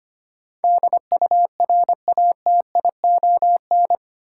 Morse code, heard as a keyed tone, about 25 wpm, 700 Hz. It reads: DURATION
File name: Eight Letter Words - 25wpm